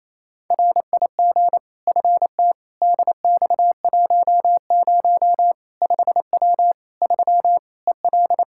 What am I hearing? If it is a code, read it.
Morse code, heard as a keyed tone, about 28 wpm, 700 Hz. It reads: RIG FT DX10 5W 3 EL
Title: Practice QSO - Exchange 2 - Info - 28wpm